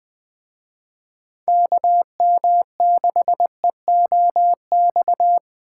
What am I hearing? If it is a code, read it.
Morse code, heard as a keyed tone, about 20 wpm, 700 Hz. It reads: KM6EOX